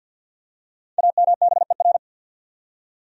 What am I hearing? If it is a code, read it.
Morse code, heard as a keyed tone, about 52 wpm, 700 Hz. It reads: AMBER
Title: USA Common Names - 50wpm